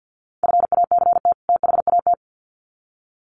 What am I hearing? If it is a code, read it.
Morse code, heard as a keyed tone, about 50 wpm, 700 Hz. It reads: FACT THAT
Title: Sets of 2 Words 50wpm